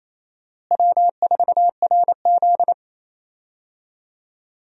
Morse code, heard as a keyed tone, about 29 wpm, 700 Hz. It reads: W4RZ
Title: USA Call Signs - 4 Characters - Repeated 3x - 28wpm